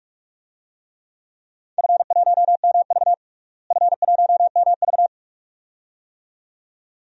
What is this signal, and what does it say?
Morse code, heard as a keyed tone, about 44 wpm, 700 Hz. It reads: F1MV F1MV
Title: DXCC-100 Call Signs - 4 Characters - Repeated 2x - 45wpm